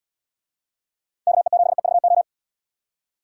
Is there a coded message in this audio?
Yes: Morse code from 1.3 to 2.2 s, heard as a keyed tone, about 77 wpm, 700 Hz. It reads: KE7EVO